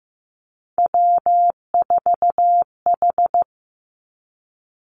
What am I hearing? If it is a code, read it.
Morse code, heard as a keyed tone, about 15 wpm, 700 Hz. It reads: W4H
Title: Three Challenging Letter-Number Combinations 15wpm